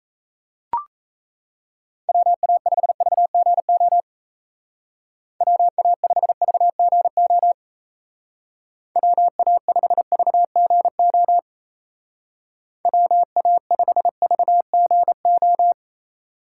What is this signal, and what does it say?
Morse code, heard as a keyed tone, about 36 wpm, 700 Hz. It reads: WA5VGO WA5VGO WA5VGO WA5VGO